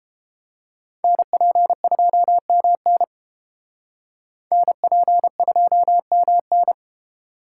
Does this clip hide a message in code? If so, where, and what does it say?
Yes: Morse code from 1.0 to 6.7 s, heard as a keyed tone, about 29 wpm, 700 Hz. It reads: NP2MN NP2MN